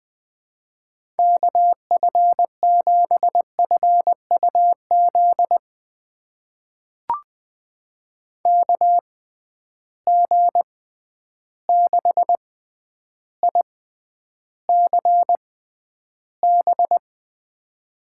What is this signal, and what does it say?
Morse code, heard as a keyed tone, about 20 wpm, spaced as for 10 wpm, 700 Hz. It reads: KF7FUZ KG6ICB
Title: USA Call Signs - 6 Characters - 6x Character Spacing - 20wpm